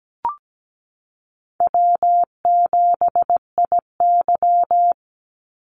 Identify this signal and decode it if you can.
Morse code, heard as a keyed tone, about 17 wpm, 700 Hz. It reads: W7IY
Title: Easy Call Signs Repeated 2x - 17wpm